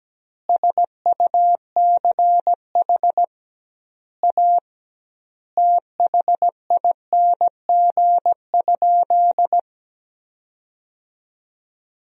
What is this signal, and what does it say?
Morse code, heard as a keyed tone, about 17 wpm, 700 Hz. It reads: SUCH A THING?
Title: 2x Word-Spacing - Sentences from Top 200 Words - 17wpm